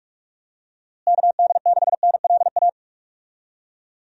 Morse code, heard as a keyed tone, about 41 wpm, 700 Hz. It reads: KD6NLA
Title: USA Call Signs - 6 Characters - Repeated 2x - 45wpm